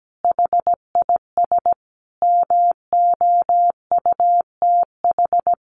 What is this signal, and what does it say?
Morse code, heard as a keyed tone, about 17 wpm, 700 Hz. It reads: HIS MOUTH
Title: Sentences from Top 1500 Words 17wpm